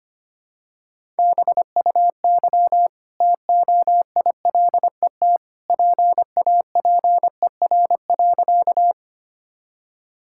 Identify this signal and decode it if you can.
Morse code, heard as a keyed tone, about 25 wpm, 700 Hz. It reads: BUY TOILET PAPER.
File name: Easy Sentences with Hints - 25wpm